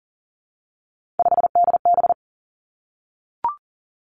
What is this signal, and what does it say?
Morse code, heard as a keyed tone, about 42 wpm, 700 Hz. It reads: 5DB
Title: Three Challenging Letter-Number Combinations 40wpm